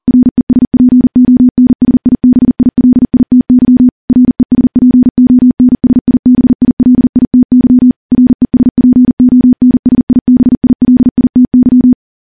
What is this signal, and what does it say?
Morse code, heard as a keyed tone, about 40 wpm, 250 Hz. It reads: RESPONSIBILITY RESPONSIBILITY RESPONSIBILITY
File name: morse / morse-training-40WPM